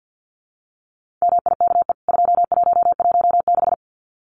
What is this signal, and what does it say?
Morse code, heard as a keyed tone, about 48 wpm, 700 Hz. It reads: MIKE 2116
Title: CWT Exchange 50wpm